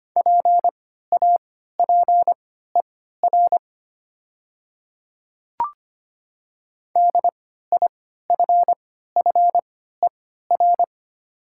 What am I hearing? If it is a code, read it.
Morse code, heard as a keyed tone, about 25 wpm, spaced as for 14 wpm, 700 Hz. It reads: PAPER DIFFER
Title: 3x Character Spacing - Top 300 Words - 25wpm